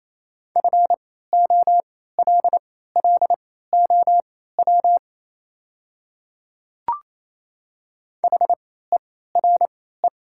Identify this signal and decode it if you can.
Morse code, heard as a keyed tone, about 28 wpm, spaced as for 16 wpm, 700 Hz. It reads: FOLLOW HERE
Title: 3x Character Spacing - Top 200 Words - 28wpm